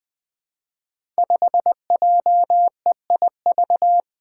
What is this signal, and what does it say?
Morse code, heard as a keyed tone, about 20 wpm, 700 Hz. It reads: 5JEIV
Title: Five Challenging Letter-Number Combinations 20wpm